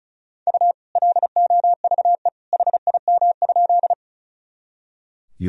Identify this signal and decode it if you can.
Morse code, heard as a keyed tone, about 36 wpm, 700 Hz. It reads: U LOVE HIM?
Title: Sentences from Top 400 Words and Abbreviations 35wpm